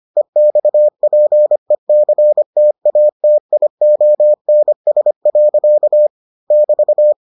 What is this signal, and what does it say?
Morse code, heard as a keyed tone, about 25 wpm, 600 Hz. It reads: EXPECTATIONS. <BT>